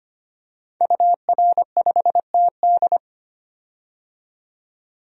Morse code, heard as a keyed tone, about 25 wpm, 700 Hz. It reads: UR5TD
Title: DXCC-100 Call Signs - 5 Characters - 25wpm